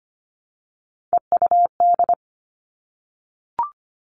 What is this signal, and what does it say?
Morse code, heard as a keyed tone, about 25 wpm, 700 Hz. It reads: EUD
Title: Three Challenging Letter-Number Combinations 25wpm